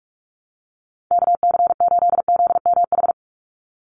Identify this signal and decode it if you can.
Morse code, heard as a keyed tone, about 44 wpm, 700 Hz. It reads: KC8ZMH
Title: USA Call Signs 45wpm